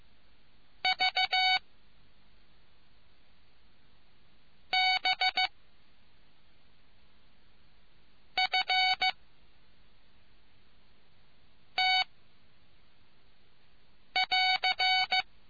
Morse code, heard as a keyed tone, about 15 wpm, 750 Hz. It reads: V B F T <AR>